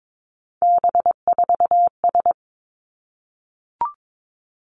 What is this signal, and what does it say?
Morse code, heard as a keyed tone, about 22 wpm, 700 Hz. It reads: B4S
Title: Three Challenging Letter-Number Combinations 22wpm